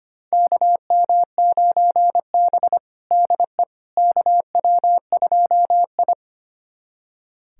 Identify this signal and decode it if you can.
Morse code, heard as a keyed tone, about 25 wpm, 700 Hz. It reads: KM9B DE KW2I
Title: USA Callsign DE Callsign - 4 Characters - 25wpm